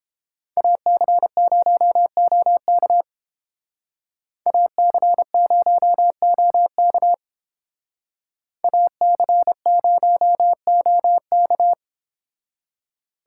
Morse code, heard as a keyed tone, about 30 wpm, 700 Hz. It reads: AC0OK AC0OK AC0OK